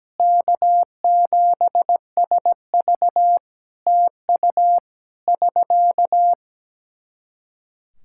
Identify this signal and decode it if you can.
Morse code, heard as a keyed tone, about 17 wpm, 700 Hz. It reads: K7SV TU <SK>
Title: Practice QSO - Exchange 4 - Ending - 17wpm